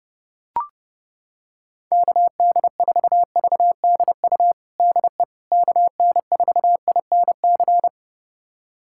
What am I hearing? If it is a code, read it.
Morse code, heard as a keyed tone, about 29 wpm, 700 Hz. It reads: KD4VDU DE KN4INC
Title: USA Callsign DE Callsign - 6 Characters - 30wpm